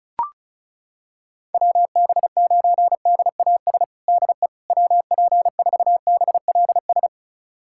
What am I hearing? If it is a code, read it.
Morse code, heard as a keyed tone, about 36 wpm, 700 Hz. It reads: WB9DAS DE WP4BLS
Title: USA Callsign DE Callsign - 6 Characters - 35wpm